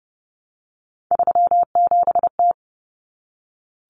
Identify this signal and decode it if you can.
Morse code, heard as a keyed tone, about 30 wpm, 700 Hz. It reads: 37T
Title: Common Cut Numbers - 3-Digit 30wpm